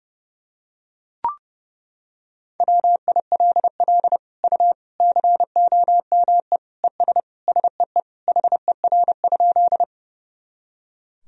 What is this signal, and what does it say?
Morse code, heard as a keyed tone, about 32 wpm, 700 Hz. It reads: WILL U COME ES SEE HER?
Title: Sentences from Top 100 Words and Abbreviations 30wpm